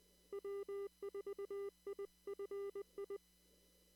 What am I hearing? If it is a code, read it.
Morse code, heard as a keyed tone, about 20 wpm, 1200 Hz. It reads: W4IFI
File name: CW / kt5j wpx cw-op-k5pi